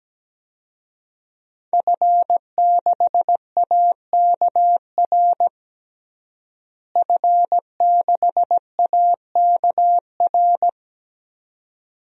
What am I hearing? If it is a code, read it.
Morse code, heard as a keyed tone, about 17 wpm, 700 Hz. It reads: F6AKR F6AKR
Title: DXCC-100 Call Signs - 5 Characters - Repeated 2x - 17wpm